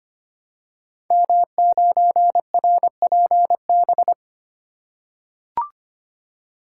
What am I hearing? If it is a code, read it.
Morse code, heard as a keyed tone, about 25 wpm, 700 Hz. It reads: M9RPB
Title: Code Groups - 5 Letter:Numbers - 25wpm